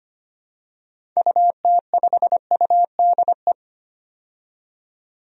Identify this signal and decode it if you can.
Morse code, heard as a keyed tone, about 25 wpm, 700 Hz. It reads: UT5UDE
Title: DXCC-100 Call Signs - 6 Characters - 25wpm